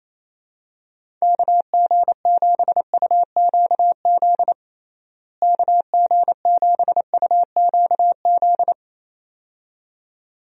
Morse code, heard as a keyed tone, about 27 wpm, 700 Hz. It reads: KG7UQZ KG7UQZ
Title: USA Call Signs - 6 Characters - Repeated 2x - 28wpm